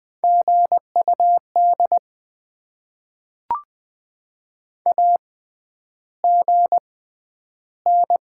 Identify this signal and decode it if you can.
Morse code, heard as a keyed tone, about 20 wpm, spaced as for 10 wpm, 700 Hz. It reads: GUD AGN
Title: Top 20 QSO Elements - 6x Character Spacing - 20wpm